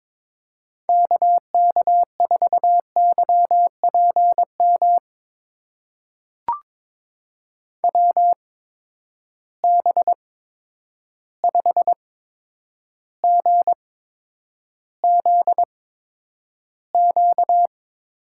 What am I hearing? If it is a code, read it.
Morse code, heard as a keyed tone, about 21 wpm, spaced as for 9 wpm, 700 Hz. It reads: KK4YPM WB5GZQ
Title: USA Call Signs - 6 Characters - 8x Character Spacing - 22wpm